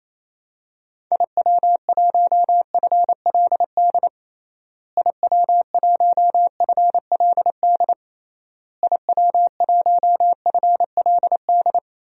Morse code, heard as a keyed tone, about 28 wpm, 700 Hz. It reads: IW1FLD IW1FLD IW1FLD
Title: DXCC-100 Call Signs - 6 Characters - Repeated 3x - 28wpm